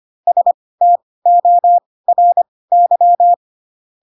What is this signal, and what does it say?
Morse code, heard as a keyed tone, about 25 wpm, spaced as for 18 wpm, 700 Hz. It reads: STORY
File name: Pinocchio / Pinocchio ch4 12 25 ik1hge